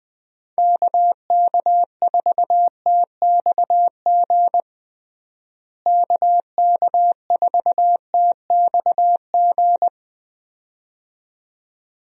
Morse code, heard as a keyed tone, about 21 wpm, 700 Hz. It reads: KK4TXG KK4TXG
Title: USA Call Signs - 6 Characters - Repeated 2x - 20wpm